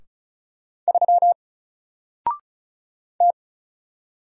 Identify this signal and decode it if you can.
Morse code, heard as a keyed tone, about 34 wpm, 700 Hz. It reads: 3 T
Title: Single Letter-Number Alt 35wpm